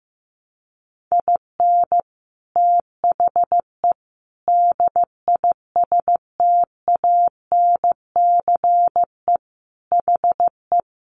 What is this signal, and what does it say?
Morse code, heard as a keyed tone, about 15 wpm, 700 Hz. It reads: IN THE DISTANCE HE
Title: Sentences from Top 1200 Words 15wpm